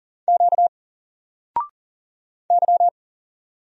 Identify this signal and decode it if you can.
Morse code, heard as a keyed tone, about 39 wpm, 700 Hz. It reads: Q Y
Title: Single Letter-Number 40wpm